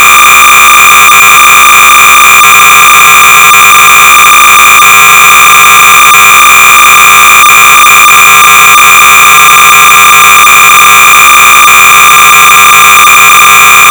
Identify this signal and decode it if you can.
Cassette data (Kansas City type).
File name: sorcerer / 300baud / Demo1-Savings and Loan-(1979)(Paul.E.Johnson)-BASIC-Sorcerer-300baud